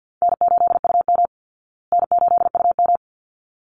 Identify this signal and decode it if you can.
Morse code, heard as a keyed tone, about 50 wpm, 700 Hz. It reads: N8UM N8UM
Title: Call Signs Repeated 3x - 50wpm